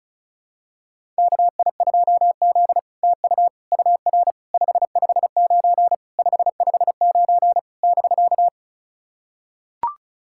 Morse code, heard as a keyed tone, about 37 wpm, 700 Hz. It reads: KI2Z TU UR 559 559 <BK>
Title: SOTA QSO Practice 35wpm